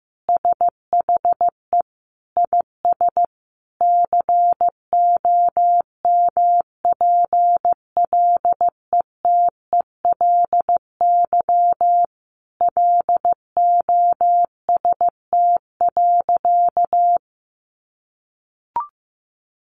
Morse code, heard as a keyed tone, about 15 wpm, 700 Hz. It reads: SHE IS COMPLETELY LOST.
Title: Additional Easy Phrases 15wpm